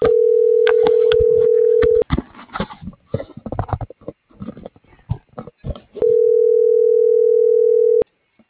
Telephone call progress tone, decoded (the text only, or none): ringback tone